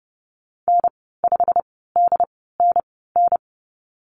Morse code, heard as a keyed tone, about 30 wpm, spaced as for 17 wpm, 700 Hz. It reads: N5DNN
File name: Lesson 15g - Number 5 - Call Signs 30wpm